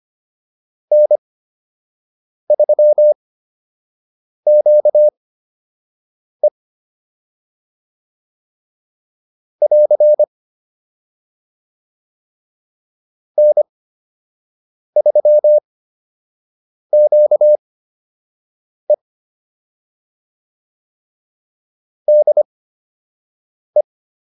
Morse code, heard as a keyed tone, about 25 wpm, spaced as for 6 wpm, 600 Hz. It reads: N3QE <AR> N3QE DE